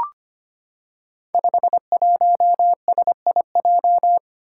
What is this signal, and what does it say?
Morse code, heard as a keyed tone, about 25 wpm, 700 Hz. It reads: 51SIJ
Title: Five Challenging Letter-Number Combinations 25wpm